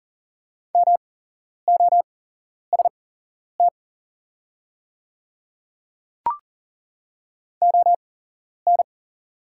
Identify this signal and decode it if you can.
Morse code, heard as a keyed tone, about 37 wpm, spaced as for 11 wpm, 700 Hz. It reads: MOST ON
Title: 8x Character Spacing - Top 200 Words - 40wpm